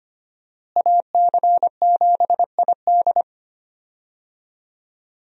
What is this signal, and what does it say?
Morse code, heard as a keyed tone, about 25 wpm, 700 Hz. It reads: AC7ID
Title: USA Call Signs - 5 Characters - Repeated 2x - 25wpm